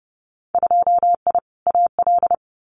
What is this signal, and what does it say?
Morse code, heard as a keyed tone, about 30 wpm, 700 Hz. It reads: 2I AL
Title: Winter Field Day Exchanges 30wpm